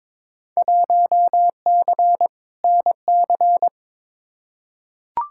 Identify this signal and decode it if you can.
Morse code, heard as a keyed tone, about 22 wpm, 700 Hz. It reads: 1C NC
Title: Field Day Exchanges - 22wpm